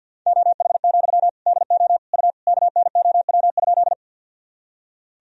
Morse code, heard as a keyed tone, about 48 wpm, 700 Hz. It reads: OH, DO U KNOW?
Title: Sentences from Top 500 Words and Abbreviations 50wpm